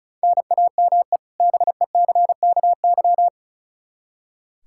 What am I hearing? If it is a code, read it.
Morse code, heard as a keyed tone, about 36 wpm, 700 Hz. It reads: NAME BECKY
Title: Practice QSO - Exchange 1 - Beginning - 35wpm